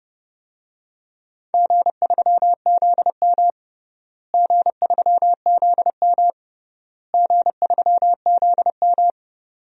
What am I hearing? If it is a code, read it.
Morse code, heard as a keyed tone, about 30 wpm, 700 Hz. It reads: G3ZM G3ZM G3ZM
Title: DXCC-100 Call Signs - 4 Characters - Repeated 3x - 30wpm